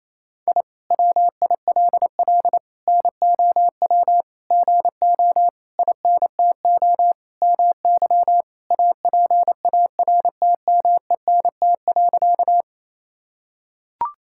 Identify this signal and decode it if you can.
Morse code, heard as a keyed tone, about 27 wpm, 700 Hz. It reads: I WILL NOW GO INTO MY APARTMENT.
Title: Sentences from Top 1000 Words - Review Part 6 - 28wpm